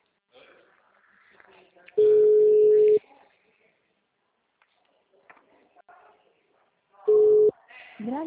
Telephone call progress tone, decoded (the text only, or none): ringback tone